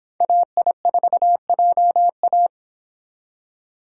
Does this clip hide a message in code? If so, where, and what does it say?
Yes: Morse code from 0.2 to 2.5 s, heard as a keyed tone, about 26 wpm, 700 Hz. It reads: AI4JA